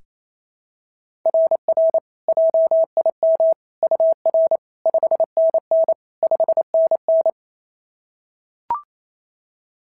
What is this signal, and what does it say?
Morse code, heard as a keyed tone, about 25 wpm, 650 Hz. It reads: RR JIM UR 5NN 5NN
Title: SOTA QSO Practice 28wpm